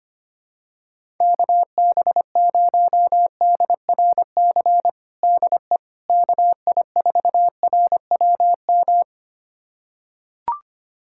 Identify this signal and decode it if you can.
Morse code, heard as a keyed tone, about 25 wpm, 700 Hz. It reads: KB0DRC DE KI4RWM
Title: USA Callsign DE Callsign - 6 Characters - 25wpm